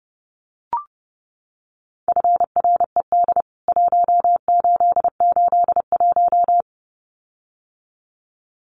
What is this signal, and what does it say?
Morse code, heard as a keyed tone, about 30 wpm, 700 Hz. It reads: FRED 1881
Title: CWT Exchange 30wpm